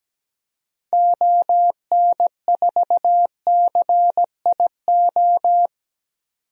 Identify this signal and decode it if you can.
Morse code, heard as a keyed tone, about 17 wpm, 700 Hz. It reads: ON4CIO
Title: DXCC-100 Call Signs - 6 Characters - 17wpm